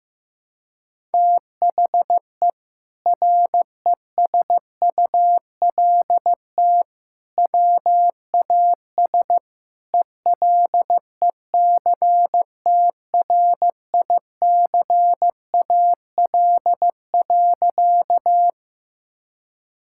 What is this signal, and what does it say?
Morse code, heard as a keyed tone, about 15 wpm, 700 Hz. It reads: THE RESULT WAS ELECTRICAL.
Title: Sentences from Top 1000 Words - Review Part 6 - 15wpm